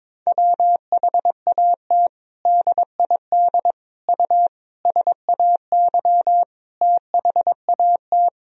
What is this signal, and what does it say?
Morse code, heard as a keyed tone, about 22 wpm, 700 Hz. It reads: WHAT DID U SAY THAT
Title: Sentences from Top 200 Words and Abbreviations 22wpm